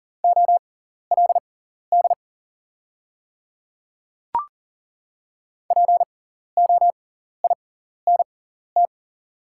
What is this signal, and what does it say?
Morse code, heard as a keyed tone, about 40 wpm, spaced as for 14 wpm, 700 Hz. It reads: OLD POINT
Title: 6x Character Spacing - Top 300 Words - 40wpm